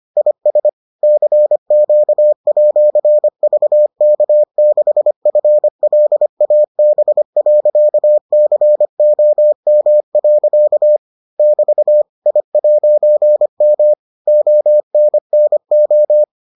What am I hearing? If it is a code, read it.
Morse code, heard as a keyed tone, about 25 wpm, 600 Hz. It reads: IS CQ@VK6FLAB.COM. <BT> I'M ONNO